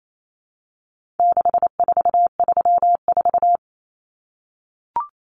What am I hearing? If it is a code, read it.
Morse code, heard as a keyed tone, about 28 wpm, 700 Hz. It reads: 6434
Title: Numbers - 4-Digit 28wpm